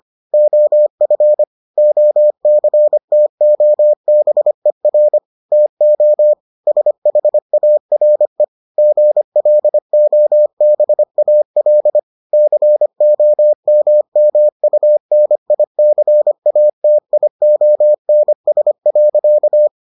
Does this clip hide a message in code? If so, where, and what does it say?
Yes: Morse code from 0.3 to 19.7 s, heard as a keyed tone, about 25 wpm, 600 Hz. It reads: OF OCTOBER TO SHARE GLOBAL COMMUNICATIONS.